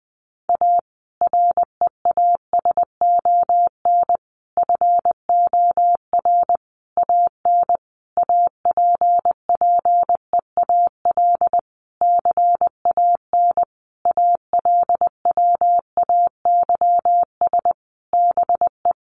Morse code, heard as a keyed tone, about 20 wpm, 700 Hz. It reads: A REASON FOR AN APPEAL CAN ALWAYS BE
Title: Sentences from Top 1300 Words 20wpm